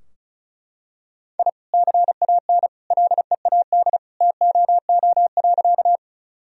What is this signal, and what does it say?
Morse code, heard as a keyed tone, about 36 wpm, 700 Hz. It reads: I CAN LEAD TOO.
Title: Sentences from Top 1000 Words - Review Part 2 - 35wpm